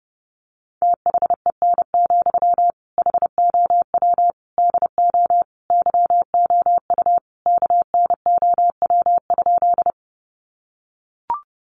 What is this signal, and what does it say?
Morse code, heard as a keyed tone, about 30 wpm, 700 Hz. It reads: THEN, HOW DO YOU KNOW?
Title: Sentences from Top 100 Words 30wpm